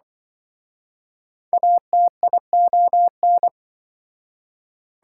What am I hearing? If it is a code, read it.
Morse code, heard as a keyed tone, about 25 wpm, 700 Hz. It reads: ATION